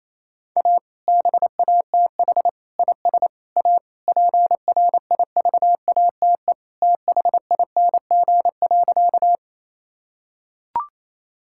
Morse code, heard as a keyed tone, about 28 wpm, 700 Hz. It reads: A BATH IS A PRIVATE THING.
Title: Sentences from Top 1600 Words 28wpm